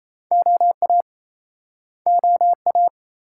Morse code, heard as a keyed tone, about 40 wpm, 700 Hz. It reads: OA OA